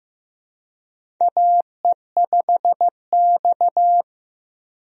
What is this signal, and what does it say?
Morse code, heard as a keyed tone, about 15 wpm, 700 Hz. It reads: AE5X
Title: USA Call Signs - 4 Characters - 15wpm